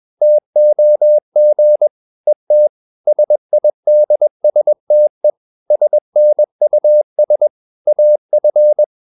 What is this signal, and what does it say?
Morse code, heard as a keyed tone, about 21 wpm, 600 Hz. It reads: TOG ET SIDSTE SNUS AF
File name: Ferie QRS 21wpm